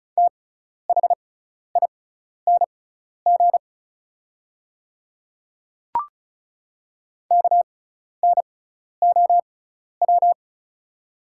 Morse code, heard as a keyed tone, about 34 wpm, spaced as for 12 wpm, 690 Hz. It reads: THING KNOW
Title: 6x Character Spacing - Top 100 Words - 35wpm